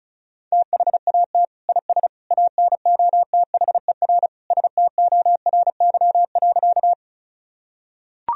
Morse code, heard as a keyed tone, about 34 wpm, 700 Hz. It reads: THAT IS ANOTHER STORY.
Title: Sentences from Top 1000 Words - Review Part 1 - 35wpm